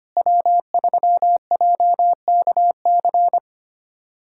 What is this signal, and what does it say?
Morse code, heard as a keyed tone, about 25 wpm, 700 Hz. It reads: W3JKC
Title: USA Callsign DE Callsign 25wpm